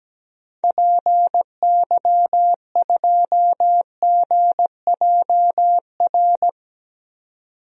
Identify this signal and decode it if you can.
Morse code, heard as a keyed tone, about 17 wpm, 700 Hz. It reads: PY2GJR
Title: DXCC-100 Call Signs - 6 Characters - 17wpm